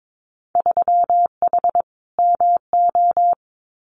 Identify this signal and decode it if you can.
Morse code, heard as a keyed tone, about 22 wpm, 700 Hz. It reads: 3H MO
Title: Winter Field Day Exchanges Encore 22wpm